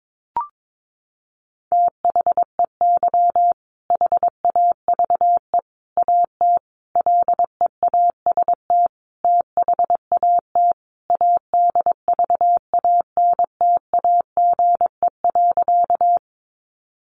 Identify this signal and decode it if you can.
Morse code, heard as a keyed tone, about 22 wpm, 700 Hz. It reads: THEY HAVE AT LEAST THAT ADVANTAGE.
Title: Sentences from Top 900 Words 22wpm